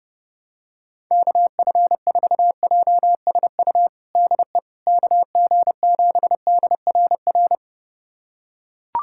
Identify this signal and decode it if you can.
Morse code, heard as a keyed tone, about 32 wpm, 690 Hz. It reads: KF4JSU DE KG7DRR
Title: USA Callsign DE Callsign 30wpm